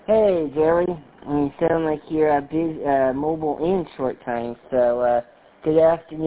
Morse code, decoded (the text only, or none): none